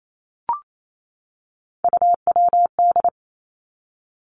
Morse code, heard as a keyed tone, about 28 wpm, 700 Hz. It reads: UWD
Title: Three Challenging Letter-Number Combinations 28wpm